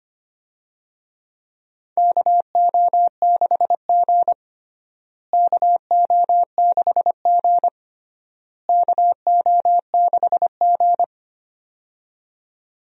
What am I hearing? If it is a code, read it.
Morse code, heard as a keyed tone, about 25 wpm, 700 Hz. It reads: KO6G KO6G KO6G
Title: USA Call Signs - 4 Characters - Repeated 3x - 25wpm